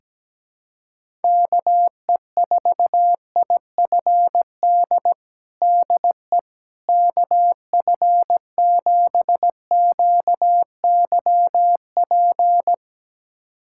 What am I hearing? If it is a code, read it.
Morse code, heard as a keyed tone, about 17 wpm, 700 Hz. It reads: KE4IFD DE KF7QYP
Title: USA Callsign DE Callsign - 6 Characters - 17wpm